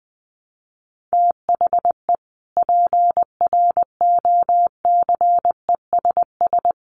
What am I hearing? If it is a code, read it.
Morse code, heard as a keyed tone, about 20 wpm, 700 Hz. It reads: THE PROCESS
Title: Sets of 3 Words 20wpm